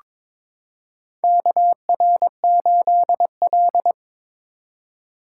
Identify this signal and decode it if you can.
Morse code, heard as a keyed tone, about 22 wpm, 700 Hz. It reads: KR8L
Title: K1USN SST QSO Practice - 22wpm